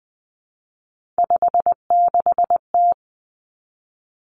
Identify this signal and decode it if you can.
Morse code, heard as a keyed tone, about 20 wpm, 700 Hz. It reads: 56T
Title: Three Challenging Letter-Number Combinations 20wpm